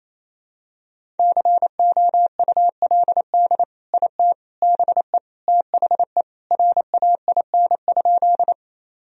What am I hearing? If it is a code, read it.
Morse code, heard as a keyed tone, about 27 wpm, 700 Hz. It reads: COULD IT BE THE RAIN?
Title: Sentences from Top 1000 Words - Review Part 1 - 28wpm